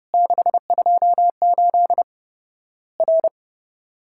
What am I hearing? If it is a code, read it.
Morse code, heard as a keyed tone, about 30 wpm, 700 Hz. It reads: 628 R
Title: Worked All Europe DX Contest - QTC Exchange - 30wpm